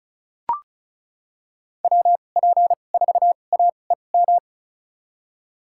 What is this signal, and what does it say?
Morse code, heard as a keyed tone, about 34 wpm, spaced as for 25 wpm, 700 Hz. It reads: WP4AEM
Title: USA Call Signs - 6 Characters - 2x Character Spacing - 35wpm